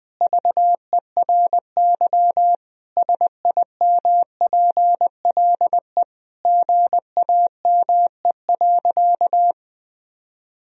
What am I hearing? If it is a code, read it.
Morse code, heard as a keyed tone, about 21 wpm, 700 Hz. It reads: VERY SIMPLE GAME.
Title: Sentences from Top 1000 Words - Review Part 3 - 20wpm